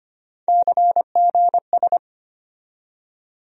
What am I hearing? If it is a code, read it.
Morse code, heard as a keyed tone, about 25 wpm, 700 Hz. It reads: CGS
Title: USA Call Sign Postfixes - 3 Letters 25wpm